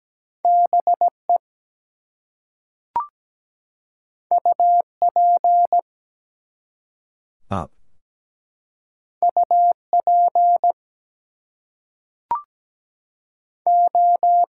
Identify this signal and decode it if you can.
Morse code, heard as a keyed tone, about 17 wpm, 700 Hz. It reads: BE UP UP O
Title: Two Letter Words - 17wpm